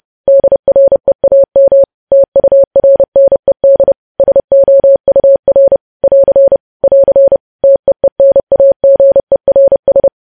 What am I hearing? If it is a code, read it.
Morse code, heard as a keyed tone, about 30 wpm, 550 Hz. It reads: DREAM TURNED SOUR <AR> <AR> TEENAGERS
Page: news headlines in morse code at 30 wpm